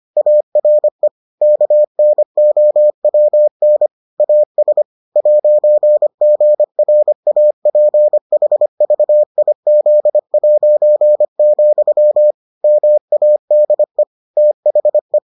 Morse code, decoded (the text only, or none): ARE KNOWN AS 'GRAPHVIZ', MADE THE